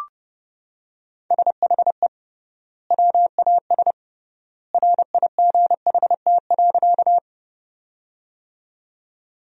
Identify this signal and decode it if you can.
Morse code, heard as a keyed tone, about 30 wpm, 700 Hz. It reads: SHE WAS RIGHT.
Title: 3x Word-Spacing - Sentences from Top 200 Words - 30wpm